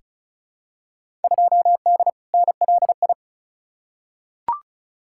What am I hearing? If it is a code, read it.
Morse code, heard as a keyed tone, about 34 wpm, 700 Hz. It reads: 2D NLI
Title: Field Day Exchanges Encore - 35wpm